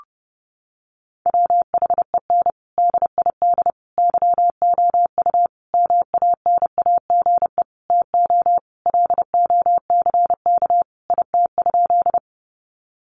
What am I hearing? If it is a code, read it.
Morse code, heard as a keyed tone, about 30 wpm, 700 Hz. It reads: WHEN DID YOU MANAGE TO LOCK IT?
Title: Sentences from Top 1400 Words 30wpm